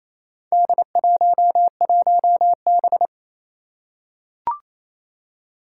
Morse code, heard as a keyed tone, about 28 wpm, 700 Hz. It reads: D11B
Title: Four Challenging Letter-Number Combinations 28wpm